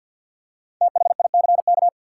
Morse code, heard as a keyed tone, about 51 wpm, 700 Hz. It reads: THICK
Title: Five Letter Words - 50wpm